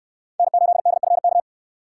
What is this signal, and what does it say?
Morse code, heard as a keyed tone, about 69 wpm, 700 Hz. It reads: N8DFK